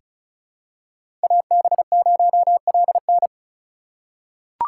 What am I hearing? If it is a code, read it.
Morse code, heard as a keyed tone, about 34 wpm, 700 Hz. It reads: AB0LN
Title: USA Call Signs - 5 Characters - 6x Character Spacing - 35wpm